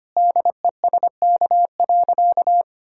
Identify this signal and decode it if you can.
Morse code, heard as a keyed tone, about 25 wpm, 700 Hz. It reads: DESK.
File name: Easy Sentences with Hints - 25wpm